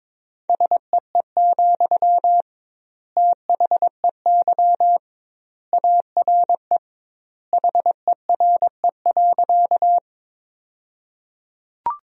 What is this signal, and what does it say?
Morse code, heard as a keyed tone, about 22 wpm, 700 Hz. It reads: SEE, THEY ARE HERE.
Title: 2x Word-Spacing - Sentences from Top 200 Words - 22wpm